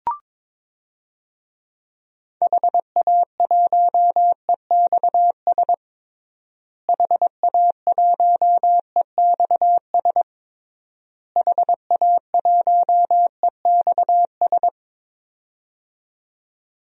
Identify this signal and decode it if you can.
Morse code, heard as a keyed tone, about 22 wpm, 700 Hz. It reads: HA1EXS HA1EXS HA1EXS
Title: DXCC-100 Call Signs - 6 Characters - Repeated 3x - 22wpm